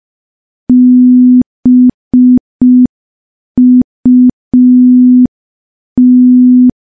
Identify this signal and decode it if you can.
Morse code, heard as a keyed tone, about 5 wpm, 250 Hz. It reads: BUT
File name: morse / morse-training-05WPM